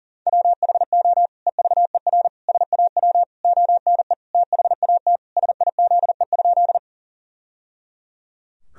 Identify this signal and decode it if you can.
Morse code, heard as a keyed tone, about 40 wpm, 700 Hz. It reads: WHO EVER SAW ONE THAT SIZE?
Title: Sentences from Top 1000 Words - Review Part 2 - 40wpm